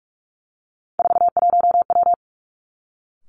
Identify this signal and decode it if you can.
Morse code, heard as a keyed tone, about 46 wpm, 700 Hz. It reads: 41W
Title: Three Challenging Letter-Number Combinations 45wpm